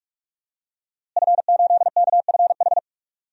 Morse code, heard as a keyed tone, about 46 wpm, 700 Hz. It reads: F8KFH
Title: DXCC-100 Call Signs - 5 Characters - Repeated 2x - 45wpm